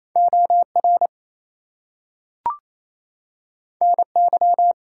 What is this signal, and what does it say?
Morse code, heard as a keyed tone, about 28 wpm, 700 Hz. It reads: OR NY
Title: US States and Canadian Provinces 28wpm